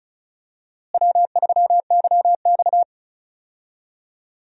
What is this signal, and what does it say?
Morse code, heard as a keyed tone, about 36 wpm, 700 Hz. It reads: W3YX
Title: USA Call Signs - 4 Characters - Repeated 3x - 35wpm